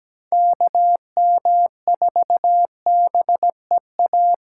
Morse code, heard as a keyed tone, about 17 wpm, 700 Hz. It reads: K M 4 B E A
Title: USA Call Signs - 6 Characters - 8x Character Spacing - 17wpm